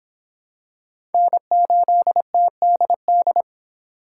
Morse code, heard as a keyed tone, about 25 wpm, 700 Hz. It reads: N8TDD